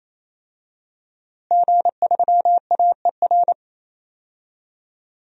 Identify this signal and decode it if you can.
Morse code, heard as a keyed tone, about 28 wpm, 700 Hz. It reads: G3AER
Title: DXCC-100 Call Signs - 5 Characters - Repeated 3x - 28wpm